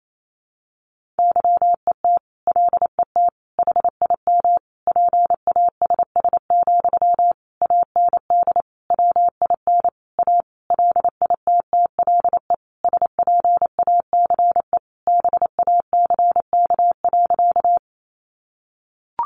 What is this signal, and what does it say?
Morse code, heard as a keyed tone, about 28 wpm, 700 Hz. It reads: YET LET HIM PASS, AND WIN A LITTLE SPACE BACK.
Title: Sentences from Top 1500 Words 28wpm